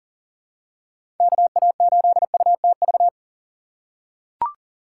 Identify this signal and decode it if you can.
Morse code, heard as a keyed tone, about 37 wpm, 700 Hz. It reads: KA8UTV
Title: USA Call Signs - 6 Characters - 2x Character Spacing - 40wpm